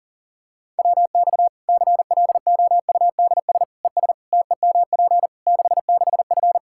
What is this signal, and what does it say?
Morse code, heard as a keyed tone, about 40 wpm, 700 Hz. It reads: WX CLOUDS ES TEMP 66F
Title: Practice QSO - Exchange 2 - Info - 40wpm